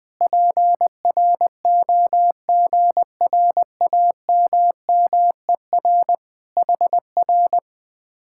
Morse code, heard as a keyed tone, about 20 wpm, 700 Hz. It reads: PROGRAMMER HR
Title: Speed-Racing Common QSO Phrases - 17wpm